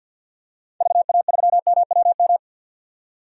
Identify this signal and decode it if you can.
Morse code, heard as a keyed tone, about 47 wpm, 700 Hz. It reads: VA3MWM
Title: DXCC-100 Call Signs - 6 Characters - 50wpm